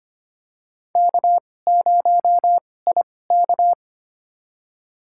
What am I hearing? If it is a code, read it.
Morse code, heard as a keyed tone, about 25 wpm, spaced as for 18 wpm, 700 Hz. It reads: K0IK
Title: USA Call Signs - 2x Character Spacing - 25wpm